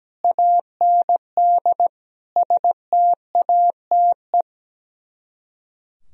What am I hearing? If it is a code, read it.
Morse code, heard as a keyed tone, about 17 wpm, 700 Hz. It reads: AND STATE
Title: Speed-Racing - Sets of 5 Words 17wpm